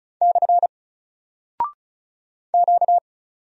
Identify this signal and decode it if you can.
Morse code, heard as a keyed tone, about 36 wpm, 700 Hz. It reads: / Q
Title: Single Character (Letter:Number:Punctuation:Prosign) 35wpm